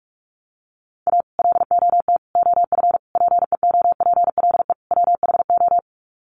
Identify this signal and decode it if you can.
Morse code, heard as a keyed tone, about 44 wpm, 700 Hz. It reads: A LOT OF PEOPLE WHO
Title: Sets of 5 Words 45wpm